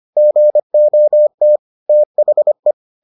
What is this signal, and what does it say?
Morse code, heard as a keyed tone, about 25 wpm, 600 Hz. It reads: GOT THE